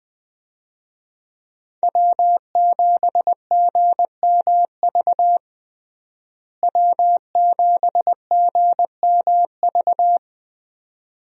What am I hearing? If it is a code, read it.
Morse code, heard as a keyed tone, about 20 wpm, 700 Hz. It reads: W7GMV W7GMV
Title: USA Call Signs - 5 Characters - Repeated 2x - 20wpm